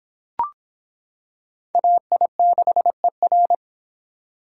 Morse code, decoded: AI6ER

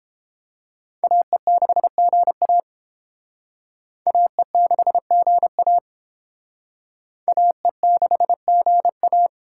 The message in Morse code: AE6GA AE6GA AE6GA

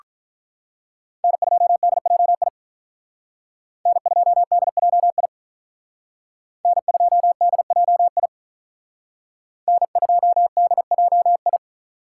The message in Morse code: N2DJI N2DJI N2DJI N2DJI

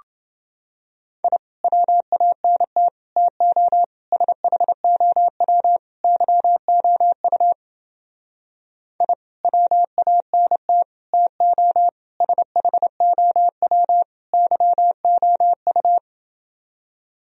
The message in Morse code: I WANT TO SHOW YOU I WANT TO SHOW YOU

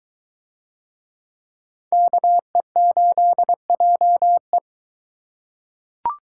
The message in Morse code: KE8JE

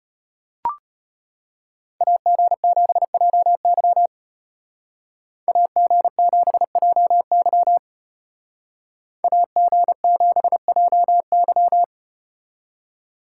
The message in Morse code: AG7JY AG7JY AG7JY

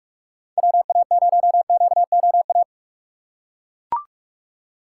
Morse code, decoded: WA0QOA

